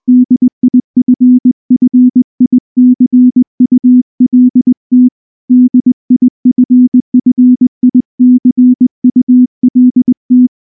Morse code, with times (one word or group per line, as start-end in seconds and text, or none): 0.1-5.1 DIFFICULT
5.5-10.5 DIFFICULT